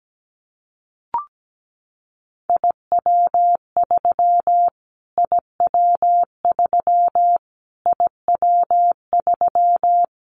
Morse code, heard as a keyed tone, about 17 wpm, 700 Hz. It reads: IW3 IW3 IW3
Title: Callsign Prefixes for Top 25 DXCC - Repeated 3x - 17wpm